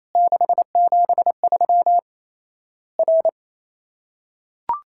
Morse code, heard as a keyed tone, about 28 wpm, 700 Hz. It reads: 673 R
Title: Worked All Europe DX Contest - QTC Exchange - 28wpm